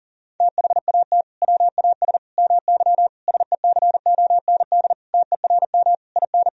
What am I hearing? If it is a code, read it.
Morse code, heard as a keyed tone, about 37 wpm, 700 Hz. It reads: THAT WAS MY SECOND TERM IN
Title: Sentences from Top 1000 Words - Review Part 4 - 40wpm